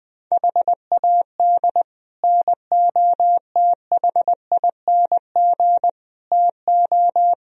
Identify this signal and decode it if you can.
Morse code, heard as a keyed tone, about 20 wpm, 700 Hz. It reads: HAD NOTHING TO
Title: Speed-Racing - Sets of 5 Words 20wpm